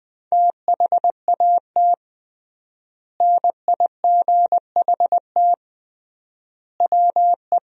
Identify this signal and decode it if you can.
Morse code, heard as a keyed tone, about 20 wpm, 700 Hz. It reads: THAT NIGHT WE
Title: 3x Word-Spacing - Sentences from Top 300 Words - 20wpm